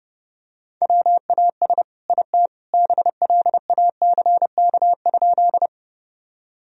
Morse code, heard as a keyed tone, about 30 wpm, 700 Hz. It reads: WAS IT BLACK?
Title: Sentences from Top 1000 Words - Review Part 2 - 30wpm